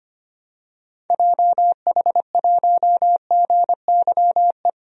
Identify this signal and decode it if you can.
Morse code, heard as a keyed tone, about 25 wpm, 700 Hz. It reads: JH1GYE
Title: DXCC-100 Call Signs - 6 Characters - 25wpm